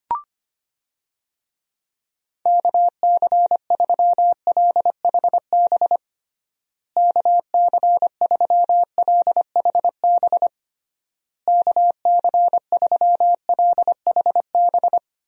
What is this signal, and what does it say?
Morse code, heard as a keyed tone, about 25 wpm, 700 Hz. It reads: KC3LHB KC3LHB KC3LHB
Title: USA Call Signs - 6 Characters - Repeated 3x - 25wpm